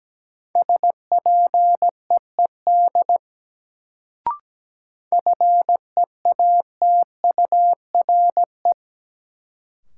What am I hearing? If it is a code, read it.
Morse code, heard as a keyed tone, about 17 wpm, 700 Hz. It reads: SPEED FEATURE
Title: Sentences from Top 800 Words - New Word Drill - 17wpm